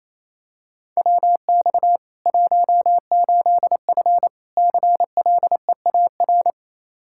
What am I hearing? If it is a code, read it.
Morse code, heard as a keyed tone, about 28 wpm, 700 Hz. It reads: WX 18F CLEAR
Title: Practice QSO - Exchange 2 - Info - 28wpm